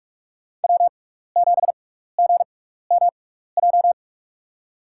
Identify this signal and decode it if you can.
Morse code, heard as a keyed tone, about 42 wpm, spaced as for 15 wpm, 690 Hz. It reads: W7GMJ
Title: USA Call Signs - 5 Characters - 6x Character Spacing - 45wpm